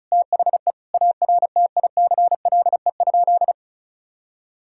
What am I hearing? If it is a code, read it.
Morse code, heard as a keyed tone, about 34 wpm, 700 Hz. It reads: THE ARTICLE?
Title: Sentences from Top 1000 Words - Review Part 5 - 35wpm